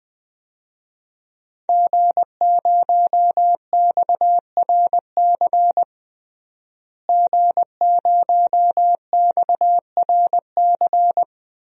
Morse code, heard as a keyed tone, about 20 wpm, 700 Hz. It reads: G0XRC G0XRC
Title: DXCC-100 Call Signs - 5 Characters - Repeated 2x - 20wpm